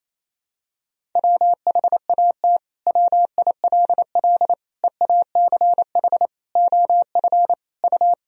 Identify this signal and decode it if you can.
Morse code, heard as a keyed tone, about 28 wpm, 700 Hz. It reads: WHAT WILL EACH OF U
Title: Sentences from Top 200 Words and Abbreviations 28wpm